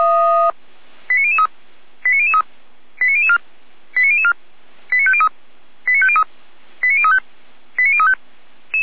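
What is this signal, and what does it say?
Five-tone selective call (ZVEI), heard as quick runs of five tones, tones of about 80 ms. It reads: 89003 89003 89004 89004 89503 89503 89035 89035
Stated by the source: ZVEI five-tone call received by SDR, from the server of Feuerwehr Orlamünde (fire brigade).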